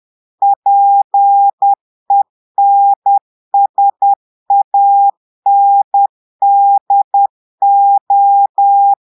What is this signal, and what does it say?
Morse code, heard as a keyed tone, about 10 wpm, 800 Hz. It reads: PENSANDO
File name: portugais texte en clair 2 vitesse 10 mots minutes